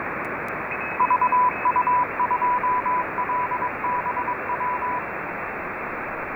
Morse code, heard as a keyed tone, about 22 wpm, 990 Hz, with radio noise. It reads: VU2LBW
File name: RS44-VU2LBW